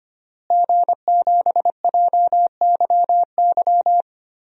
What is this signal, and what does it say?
Morse code, heard as a keyed tone, about 25 wpm, 700 Hz. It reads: G7JYY
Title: DXCC-100 Call Signs - 5 Characters - Repeated 2x - 25wpm